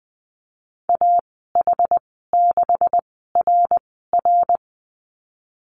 Morse code, heard as a keyed tone, about 19 wpm, spaced as for 14 wpm, 700 Hz. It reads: AH6RR
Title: 128 Lesson 21g - Number 6 - Call Signs 20wpm